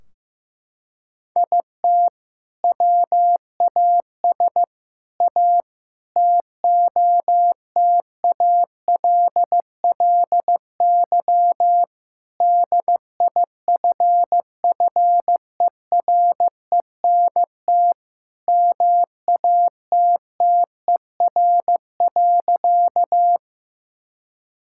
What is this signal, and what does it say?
Morse code, heard as a keyed tone, about 15 wpm, 700 Hz. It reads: IT WAS A TOTALLY DIFFERENT MATTER.
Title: Sentences from Top 1600 Words 15wpm